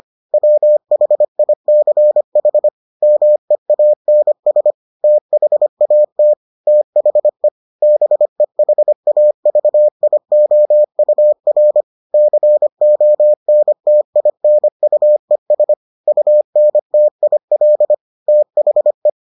Morse code, heard as a keyed tone, about 25 wpm, 600 Hz. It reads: WHICH MEANS THAT THE BEHAVIOUR CONTINUES UNTIL THE